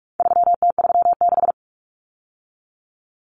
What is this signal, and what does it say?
Morse code, heard as a keyed tone, about 44 wpm, 700 Hz. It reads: 3T36
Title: Common Cut Numbers - 4-Digit 45wpm